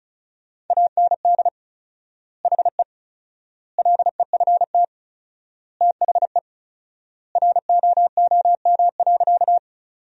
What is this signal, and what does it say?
Morse code, heard as a keyed tone, about 36 wpm, 700 Hz. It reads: AND HE LEFT THE ROOM.
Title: 4x Word-Spacing - Sentences from Top 300 Words - 35wpm